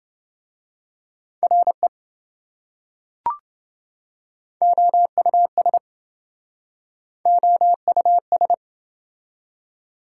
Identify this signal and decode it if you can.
Morse code, heard as a keyed tone, about 27 wpm, 700 Hz. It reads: RE OUS OUS